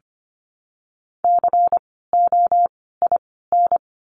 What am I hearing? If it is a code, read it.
Morse code, heard as a keyed tone, about 25 wpm, 700 Hz. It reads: C O I N
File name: course / 25:10wpm / 094 Lesson 16f - Letter c - Words 25wpm